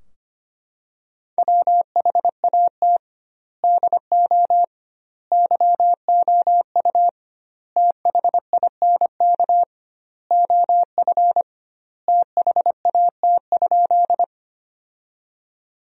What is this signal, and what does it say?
Morse code, heard as a keyed tone, about 25 wpm, 700 Hz. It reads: WHAT DO YOU THINK OF THAT?
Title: 2x Word-Spacing - Sentences from Top 200 Words - 25wpm